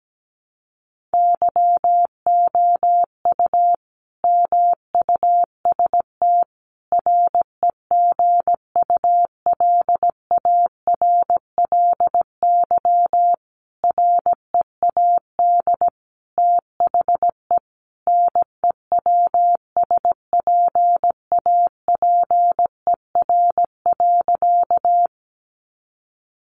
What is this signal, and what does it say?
Morse code, heard as a keyed tone, about 17 wpm, 700 Hz. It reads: YOU MUST REGULARLY READ THE NEWSPAPER.
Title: Sentences from Top 1400 Words 17wpm